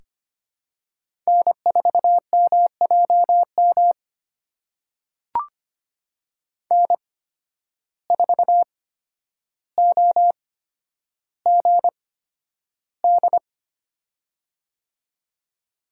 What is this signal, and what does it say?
Morse code, heard as a keyed tone, about 25 wpm, spaced as for 11 wpm, 700 Hz. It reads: N4MJM N4OGD
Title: USA Call Signs - 5 Characters - 8x Character Spacing - 25wpm